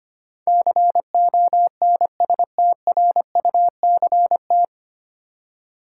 Morse code, heard as a keyed tone, about 25 wpm, 700 Hz. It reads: CONSTRUCT
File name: practice / 25wpm / 9 Letter Words - First 3 Letters Given - 25wpm